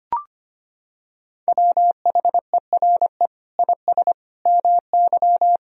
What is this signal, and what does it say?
Morse code, heard as a keyed tone, about 25 wpm, 700 Hz. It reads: WHERE IS MY
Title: Sentences from Top 1000 Words - Review Part 1 - 25wpm